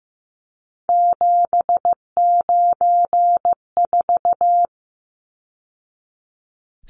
Morse code, heard as a keyed tone, about 15 wpm, 700 Hz. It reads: 794
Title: Numbers - 3-Digit 15wpm